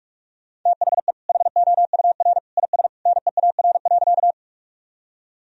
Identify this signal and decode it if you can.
Morse code, heard as a keyed tone, about 46 wpm, 700 Hz. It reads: THE HOUR IS NEAR.
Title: Sentences from Top 1000 Words - Review Part 1 - 45wpm